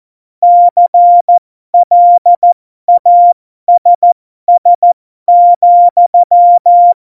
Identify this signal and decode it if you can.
Morse code, heard as a keyed tone, about 14 wpm, spaced as for 12 wpm, 700 Hz. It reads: CLASS,
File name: Morsekurs / Witze